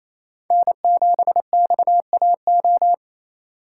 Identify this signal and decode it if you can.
Morse code, heard as a keyed tone, about 34 wpm, 700 Hz. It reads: N7XAO
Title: Speed Racing - USA 1x3 Call Signs 28wpm